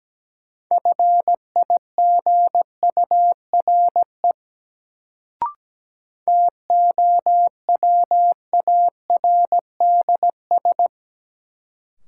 Morse code, heard as a keyed tone, about 17 wpm, 700 Hz. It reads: FIGURE TOWARDS
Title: Sentences from Top 1000 Words - Review Part 2 - New Word Drill - 17wpm